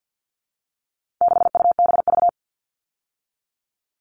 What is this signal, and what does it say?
Morse code, heard as a keyed tone, about 50 wpm, 700 Hz. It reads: 6UBV
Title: All Cut Numbers - 4-Digit 50wpm